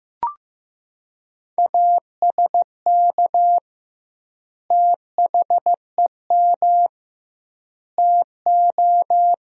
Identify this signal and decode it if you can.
Morse code, heard as a keyed tone, about 15 wpm, 700 Hz. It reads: ASK THEM TO
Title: 2x Word-Spacing - Sentences from Top 400 Words - 15wpm